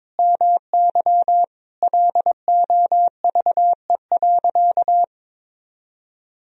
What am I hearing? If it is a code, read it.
Morse code, heard as a keyed tone, about 22 wpm, 700 Hz. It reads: MY LOVE.
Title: Sentences from Top 400 Words and Abbreviations 22wpm